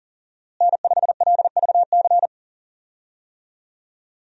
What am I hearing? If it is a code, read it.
Morse code, heard as a keyed tone, about 40 wpm, 700 Hz. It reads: N5LVC
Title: USA Call Signs - 5 Characters - Repeated 2x - 40wpm